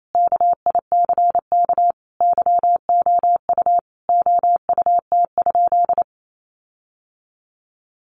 Morse code, heard as a keyed tone, about 28 wpm, 700 Hz. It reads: KICK YOU OUT?
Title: Sentences from Top 1300 Words 28wpm